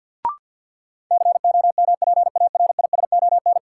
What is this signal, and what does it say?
Morse code, heard as a keyed tone, about 49 wpm, 700 Hz. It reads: COMPARISON
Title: Sentences from Top 1600 Words - New Word Drill Alt - 50wpm